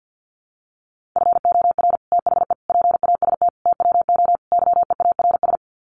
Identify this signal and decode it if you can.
Morse code, heard as a keyed tone, about 49 wpm, 700 Hz. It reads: FOR THE PAST TWO YEARS
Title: Sets of 5 Words Encore2 50wpm